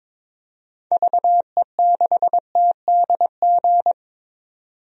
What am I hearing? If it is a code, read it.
Morse code, heard as a keyed tone, about 22 wpm, 700 Hz. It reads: VE6TDG
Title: DXCC-100 Call Signs - 6 Characters - 22wpm